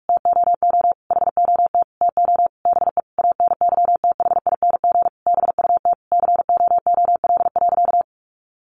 Morse code, heard as a keyed tone, about 44 wpm, 700 Hz. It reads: TOO HOT TO BE ANYTHING BUT COOL.
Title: Sentences from Top 1200 Words 45wpm